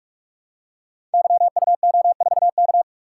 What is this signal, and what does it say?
Morse code, heard as a keyed tone, about 44 wpm, 700 Hz. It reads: YUO4K
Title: Code Groups - 5 Letter:Numbers - 45wpm